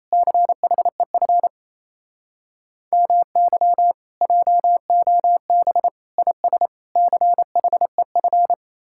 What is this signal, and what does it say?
Morse code, heard as a keyed tone, about 37 wpm, 700 Hz. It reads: CHEF MY JOB IS CHEF